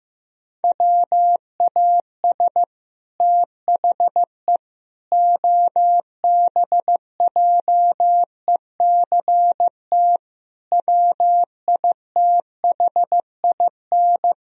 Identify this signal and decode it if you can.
Morse code, heard as a keyed tone, about 15 wpm, 700 Hz. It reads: WAS THE OBJECT WITHIN
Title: Sentences from Top 1000 Words - Review Part 4 - 15wpm